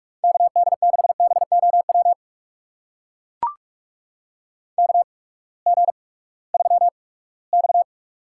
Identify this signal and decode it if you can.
Morse code, heard as a keyed tone, about 44 wpm, spaced as for 22 wpm, 700 Hz. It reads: KD6BOW KG3X
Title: USA Call Signs - 8x Character Spacing - 45wpm